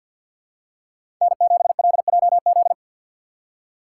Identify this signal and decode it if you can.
Morse code, heard as a keyed tone, about 50 wpm, 700 Hz. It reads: N7LJZ
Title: USA Call Signs - 5 Characters - 3x Character Spacing - 50wpm